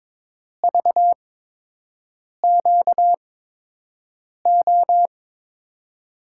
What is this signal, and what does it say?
Morse code, heard as a keyed tone, about 22 wpm, spaced as for 6 wpm, 700 Hz. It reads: VQO
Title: USA Call Signs - 6 Characters - 8x Character Spacing - 22wpm